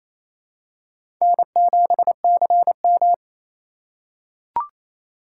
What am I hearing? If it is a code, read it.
Morse code, heard as a keyed tone, about 28 wpm, 700 Hz. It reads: N7CM
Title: USA Call Signs - 4 Characters - 6x Character Spacing - 28wpm